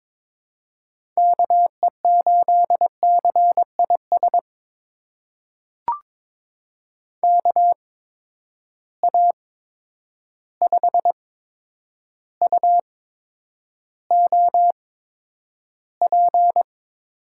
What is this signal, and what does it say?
Morse code, heard as a keyed tone, about 22 wpm, spaced as for 9 wpm, 700 Hz. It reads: KE8CIS KA5UOP
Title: USA Call Signs - 6 Characters - 8x Character Spacing - 22wpm